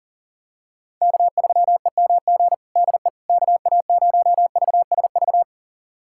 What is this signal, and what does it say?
Morse code, heard as a keyed tone, about 40 wpm, 700 Hz. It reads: K3EMG DE KA0VSV
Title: USA Callsign DE Callsign 40wpm